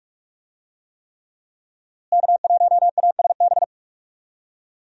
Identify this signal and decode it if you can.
Morse code, heard as a keyed tone, about 46 wpm, 700 Hz. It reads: K1ASB
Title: USA Call Signs - 5 Characters - Repeated 3x - 45wpm